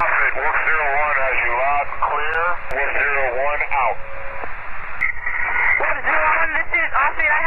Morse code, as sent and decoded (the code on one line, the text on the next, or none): none